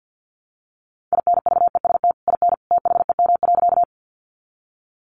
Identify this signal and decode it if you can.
Morse code, heard as a keyed tone, about 50 wpm, 700 Hz. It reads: INVEST IN THEM.
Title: Sentences from Top 1100 Words 50wpm